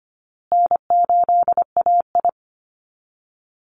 Morse code, heard as a keyed tone, about 25 wpm, 700 Hz. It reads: N8AI
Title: Call Signs Repeated 2x - 25wpm